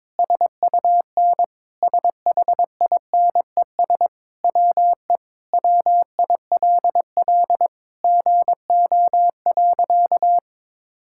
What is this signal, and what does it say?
Morse code, heard as a keyed tone, about 21 wpm, 700 Hz. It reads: SUN SHINES WE WILL GO.
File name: practice / 22wpm / Sentences with 4-Word Phrases - 22wpm